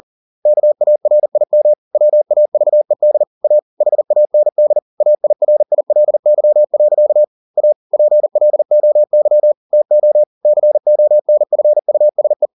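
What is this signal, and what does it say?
Morse code, heard as a keyed tone, about 40 wpm, 600 Hz. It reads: KARIM WAVED A HAND AIRILY. A PLOY TO CONFUSE
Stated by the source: Morse code at 40 wpm, practice text, CW med Gnister.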